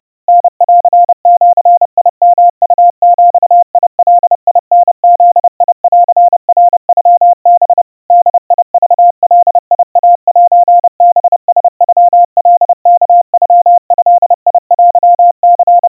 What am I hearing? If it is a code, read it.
Morse code (Japanese Wabun code), heard as a keyed tone, about 30 wpm, 700 Hz. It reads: タンジヨウビガタブンナノハ ボクガイセハラノカワノドテニ
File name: 110 cpm 700 Hz Crispy